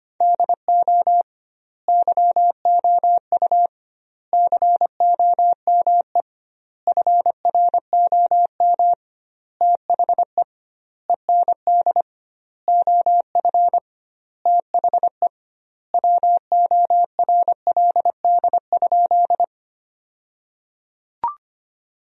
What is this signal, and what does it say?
Morse code, heard as a keyed tone, about 25 wpm, 700 Hz. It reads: DO YOU COME FROM THE END OF THE WORLD?
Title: 2x Word-Spacing - Sentences from Top 200 Words - 25wpm